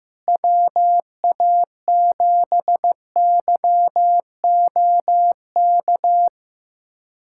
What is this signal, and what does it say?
Morse code, heard as a keyed tone, about 15 wpm, 700 Hz. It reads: WA7YOK
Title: USA Callsign DE Callsign - 6 Characters - 15wpm